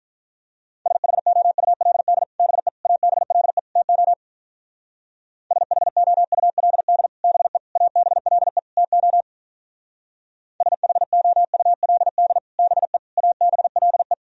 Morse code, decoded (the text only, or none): SHOULD BE ABLE TO SHOULD BE ABLE TO SHOULD BE ABLE